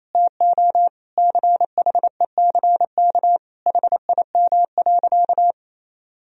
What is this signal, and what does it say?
Morse code, heard as a keyed tone, about 28 wpm, 700 Hz. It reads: TO CHECK HIM.
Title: Sentences from Top 1000 Words - Review Part 3 - 28wpm